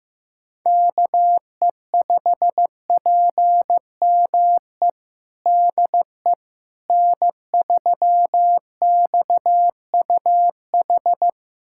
Morse code, heard as a keyed tone, about 15 wpm, 700 Hz. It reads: KE5PME DE N3XUH
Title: USA Callsign DE Callsign 15wpm